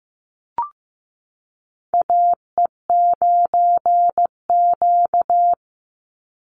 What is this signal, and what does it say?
Morse code, heard as a keyed tone, about 15 wpm, 700 Hz. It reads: AE9Q
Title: NA SOTA Chaser Callsigns 15wpm